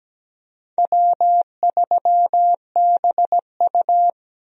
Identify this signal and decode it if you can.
Morse code, heard as a keyed tone, about 17 wpm, 700 Hz. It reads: W3BU
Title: USA Call Signs - 4 Characters - 17wpm